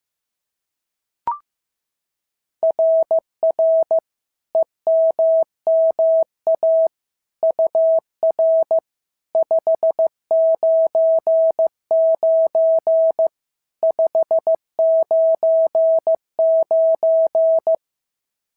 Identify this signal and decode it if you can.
Morse code, heard as a keyed tone, about 15 wpm, 650 Hz. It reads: RR EMMA UR 599 599
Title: SOTA QSO Practice 15wpm